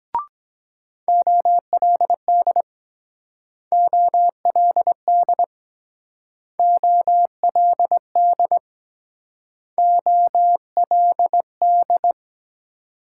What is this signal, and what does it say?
Morse code, heard as a keyed tone, about 23 wpm, 700 Hz. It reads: OLD OLD OLD OLD